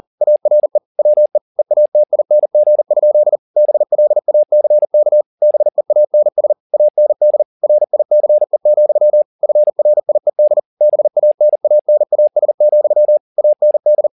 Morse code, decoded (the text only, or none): ARE WE EATING? BLACK BEANS AND RICE, FRIED BANANAS, AND